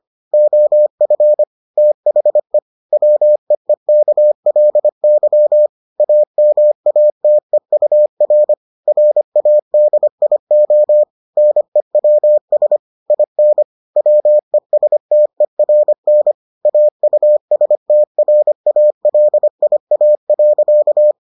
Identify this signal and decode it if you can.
Morse code, heard as a keyed tone, about 25 wpm, 600 Hz. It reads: OF THE WEEKLY AMATEUR RADIO NEWS IN WESTERN AUSTRALIA.